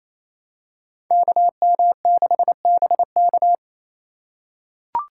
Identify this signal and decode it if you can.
Morse code, heard as a keyed tone, about 28 wpm, 700 Hz. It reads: KM6BK
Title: USA Call Signs - 5 Characters - 2x Character Spacing - 28wpm